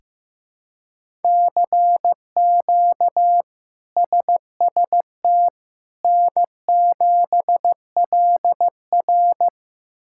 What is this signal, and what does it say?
Morse code, heard as a keyed tone, about 15 wpm, 700 Hz. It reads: CQ SST N7LR
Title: K1USN SST QSO Practice - 15wpm